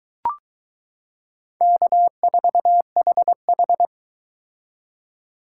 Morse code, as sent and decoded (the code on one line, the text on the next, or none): -.- ....- .... ....
K4HH